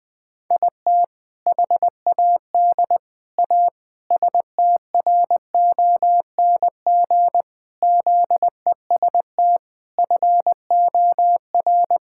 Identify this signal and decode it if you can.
Morse code, heard as a keyed tone, about 20 wpm, 700 Hz. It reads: IT HAD A STRONG ZEST FOR
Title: All-Age Appropriate Jokes - 20wpm